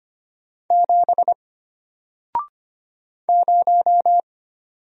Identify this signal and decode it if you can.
Morse code, heard as a keyed tone, about 25 wpm, 700 Hz. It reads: 7 0
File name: Single-Letter-Number-25wpm